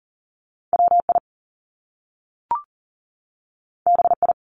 Morse code, heard as a keyed tone, about 38 wpm, 700 Hz. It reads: WI BI
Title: Top Two Letter Combinations in Words 40wpm